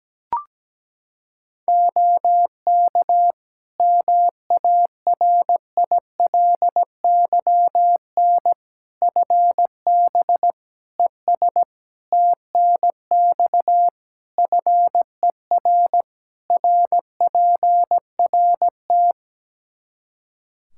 Morse code, heard as a keyed tone, about 17 wpm, 700 Hz. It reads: OK MARILYN FB ES TNX FER RPRT
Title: Practice QSO - Exchange 2 - Info - 17wpm